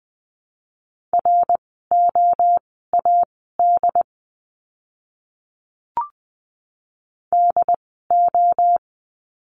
Morse code, heard as a keyed tone, about 20 wpm, spaced as for 14 wpm, 700 Hz. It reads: ROAD DO
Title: 075 Lesson 13f - Letter L - Words 20wpm